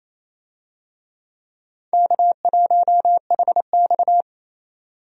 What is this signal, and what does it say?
Morse code, heard as a keyed tone, about 28 wpm, 700 Hz. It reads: K1HX
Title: USA Call Signs - 4 Characters - 28wpm